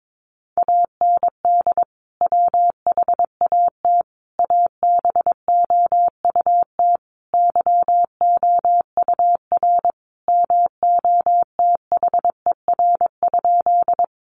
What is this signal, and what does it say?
Morse code, heard as a keyed tone, about 22 wpm, 700 Hz. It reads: AND WHAT ABOUT YOUR MOTHER?
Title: Sentences from Top 200 Words 22wpm